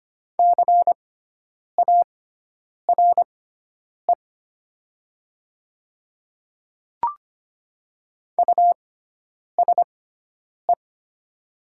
Morse code, heard as a keyed tone, about 25 wpm, spaced as for 9 wpm, 700 Hz. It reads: CARE USE
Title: 6x Character Spacing - Top 500 Words - 25wpm